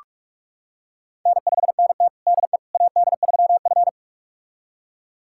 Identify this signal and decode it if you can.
Morse code, heard as a keyed tone, about 42 wpm, 700 Hz. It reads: N5NT DE AD3F
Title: USA Callsign DE Callsign - 4 Characters - 45wpm